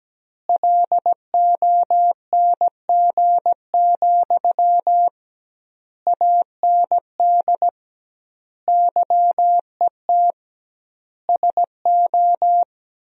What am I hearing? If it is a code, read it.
Morse code, heard as a keyed tone, about 17 wpm, 700 Hz. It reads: LONG, AND YET SO
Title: 2x Word-Spacing - Sentences from Top 500 Words - 17wpm